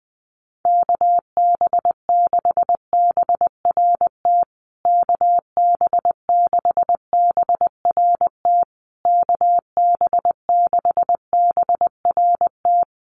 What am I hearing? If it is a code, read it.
Morse code, heard as a keyed tone, about 20 wpm, 700 Hz. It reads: KB6BRT KB6BRT KB6BRT
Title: USA Call Signs Repeated 3x 20wpm